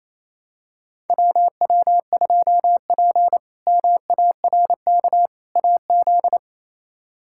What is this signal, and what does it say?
Morse code, heard as a keyed tone, about 27 wpm, 700 Hz. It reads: WW2P MARK AZ
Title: K1USN SST QSO Practice - 28wpm